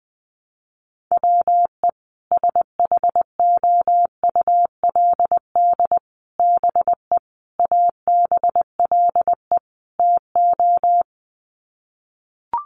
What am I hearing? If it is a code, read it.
Morse code, heard as a keyed tone, about 20 wpm, 700 Hz. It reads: WE SHOULD BE ABLE TO
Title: Sets of 5 Words Encore2 20wpm